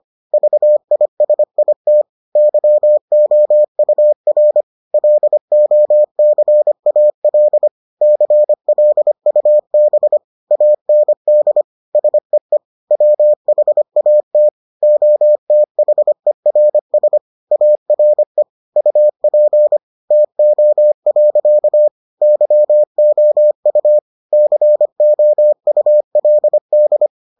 Morse code, heard as a keyed tone, about 25 wpm, 600 Hz. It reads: VISIT YOUR LOCAL CLUB AND SEE WHAT OTHERS ARE UP TO. YOU COULD